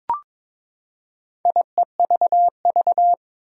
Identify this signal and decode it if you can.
Morse code, heard as a keyed tone, about 22 wpm, 700 Hz. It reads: IEVV
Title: Four Challenging Letter-Number Combinations 22wpm